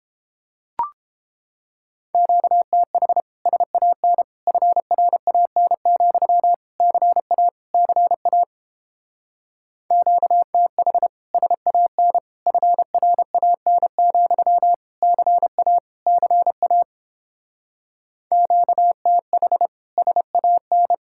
Morse code, heard as a keyed tone, about 29 wpm, 700 Hz. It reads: QTH SAN FRAN, CA CA QTH SAN FRAN, CA CA QTH SAN